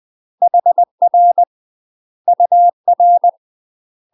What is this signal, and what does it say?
Morse code, heard as a keyed tone, about 20 wpm, 700 Hz. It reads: HR UR
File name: UK MASTER QSOs TRAINING / 20WPM / UK MASTER QSOs 0037